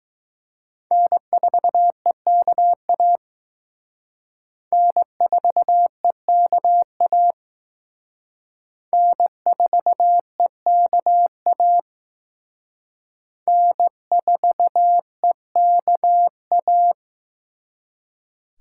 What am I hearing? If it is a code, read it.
Morse code, heard as a keyed tone, about 19 wpm, 700 Hz. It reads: N4EKA N4EKA N4EKA N4EKA